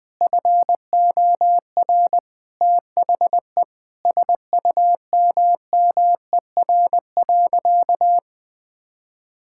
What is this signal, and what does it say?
Morse code, heard as a keyed tone, about 20 wpm, 700 Hz. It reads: FOR THE SUMMER.
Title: Sentences with 9-Word Phrases - 20wpm